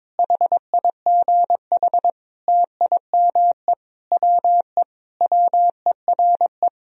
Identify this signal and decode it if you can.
Morse code, heard as a keyed tone, about 22 wpm, 700 Hz. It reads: HIGH TIME WE WERE
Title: Sentences from Top 1000 Words - Review Part 1 - 22wpm